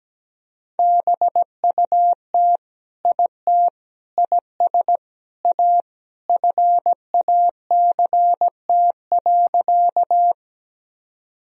Morse code, heard as a keyed tone, about 17 wpm, 700 Hz. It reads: BUT IT IS A FACT.
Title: Sentences from Top 1000 Words - Review Part 3 - 17wpm